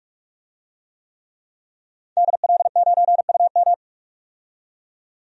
Morse code, heard as a keyed tone, about 46 wpm, 700 Hz. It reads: DL9UM
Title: DXCC-100 Call Signs - 5 Characters - 45wpm